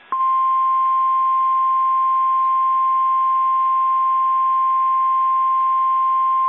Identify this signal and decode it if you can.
Emergency Alert System (SAME) tones (the 1050 Hz attention tone). It is heard from 0.9 to 5.5 s.
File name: eas / KSDZKDJL / KSDZKDJL